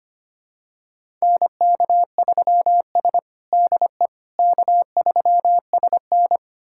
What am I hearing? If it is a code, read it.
Morse code, heard as a keyed tone, about 25 wpm, 700 Hz. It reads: NK3S DE K3SN
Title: USA Callsign DE Callsign - 4 Characters - 25wpm